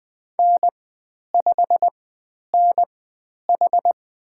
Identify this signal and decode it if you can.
Morse code, heard as a keyed tone, about 20 wpm, spaced as for 10 wpm, 700 Hz. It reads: N5NH
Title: 109 Lesson 18g - Letter m - Call Signs 20wpm